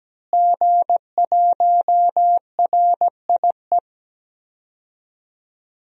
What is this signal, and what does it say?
Morse code, heard as a keyed tone, about 17 wpm, 700 Hz. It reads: G1RIE
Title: DXCC-100 Call Signs - 5 Characters - Repeated 3x - 17wpm